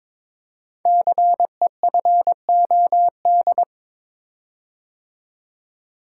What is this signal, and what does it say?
Morse code, heard as a keyed tone, about 22 wpm, 700 Hz. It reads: CEFOD
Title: Code Groups - 5 Letters - 22wpm